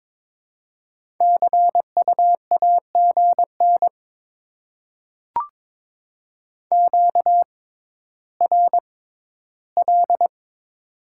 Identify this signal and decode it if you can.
Morse code, heard as a keyed tone, about 22 wpm, spaced as for 12 wpm, 700 Hz. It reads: CU AGN QRL
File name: Top 20 QSO Elements - 6x Character Spacing - 22wpm